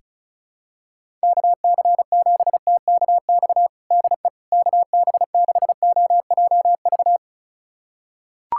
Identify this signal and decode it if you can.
Morse code, heard as a keyed tone, about 34 wpm, 700 Hz. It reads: KC7TKX DE KB6OJV
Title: USA Callsign DE Callsign - 6 Characters - 35wpm